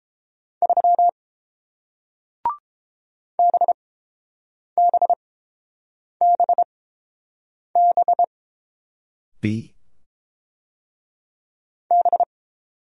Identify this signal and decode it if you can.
Morse code, heard as a keyed tone, about 30 wpm, spaced as for 7 wpm, 700 Hz. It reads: 3 BBBB B